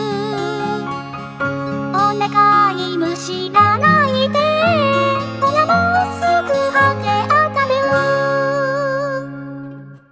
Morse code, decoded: none